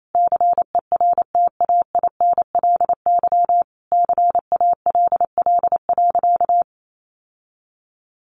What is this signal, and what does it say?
Morse code, heard as a keyed tone, about 28 wpm, 700 Hz. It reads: CERTAINLY CALL.
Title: Sentences from Top 900 Words 28wpm